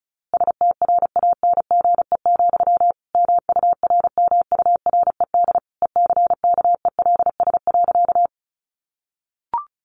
Morse code, heard as a keyed tone, about 34 wpm, 700 Hz. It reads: STRANGE, MURMURED ECKELS.
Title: A Sound Of Thunder 35wpm